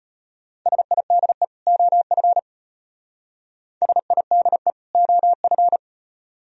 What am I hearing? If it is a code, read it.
Morse code, heard as a keyed tone, about 38 wpm, 700 Hz. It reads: SIDE OF SIDE OF